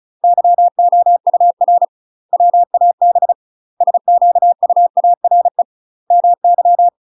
Morse code, heard as a keyed tone, about 34 wpm, 700 Hz. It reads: YOUR WAB SQUARE MY
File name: UK MASTER QSOs TRAINING / 35WPM / UK MASTER QSOs 0017